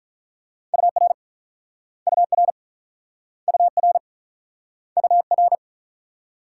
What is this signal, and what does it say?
Morse code, heard as a keyed tone, about 44 wpm, 700 Hz. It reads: UR UR UR UR